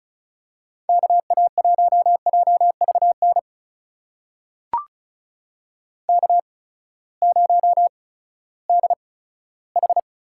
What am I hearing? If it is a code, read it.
Morse code, heard as a keyed tone, about 36 wpm, spaced as for 11 wpm, 700 Hz. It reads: KA1JV NK0DH
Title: USA Call Signs - 8x Character Spacing - 35wpm